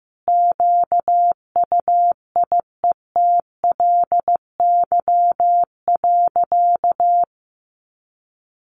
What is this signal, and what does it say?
Morse code, heard as a keyed tone, about 15 wpm, 700 Hz. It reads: QUIETLY.
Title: A Sound Of Thunder 15wpm